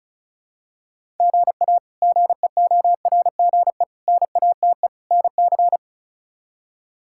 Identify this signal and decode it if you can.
Morse code, heard as a keyed tone, about 34 wpm, 700 Hz. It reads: GA GEORGE NATE NC
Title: K1USN SST QSO Practice - 35wpm